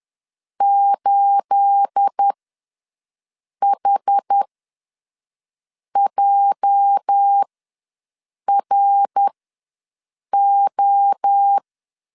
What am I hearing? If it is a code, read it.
Morse code, heard as a keyed tone, about 10 wpm, spaced as for 5 wpm, 800 Hz. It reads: 8HJRO
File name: LEZIONI / 10 SIGNS / 10A SLASH ws 10wpm 4121